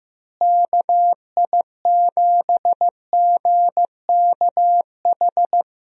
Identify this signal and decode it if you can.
Morse code, heard as a keyed tone, about 15 wpm, 700 Hz. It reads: KI7GKH
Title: USA Call Signs - 6 Characters - Repeated 2x - 15wpm